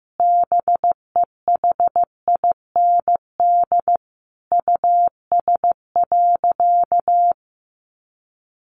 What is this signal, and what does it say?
Morse code, heard as a keyed tone, about 15 wpm, 700 Hz. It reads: BEHIND US.
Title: Sentences from Top 1300 Words 15wpm